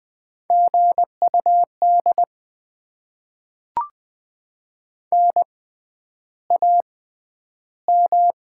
Morse code, heard as a keyed tone, about 20 wpm, spaced as for 10 wpm, 700 Hz. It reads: GUD NAM
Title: Top 20 QSO Elements - 6x Character Spacing - 20wpm